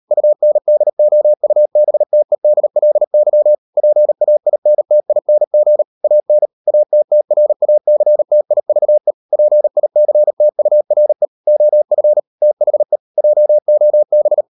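Morse code, heard as a keyed tone, about 38 wpm, 600 Hz. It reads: UNDOUBTEDLY PAINTING AN ATTRACTIVE PICTURE OF THE JOB
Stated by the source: Morse code practice at 38 WPM, sending Ken Follett's novel Never in English, from CW med Gnister.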